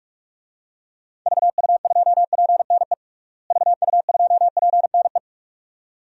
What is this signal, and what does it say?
Morse code, heard as a keyed tone, about 46 wpm, 700 Hz. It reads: VU2PNE VU2PNE
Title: DXCC-100 Call Signs - 6 Characters - Repeated 2x - 45wpm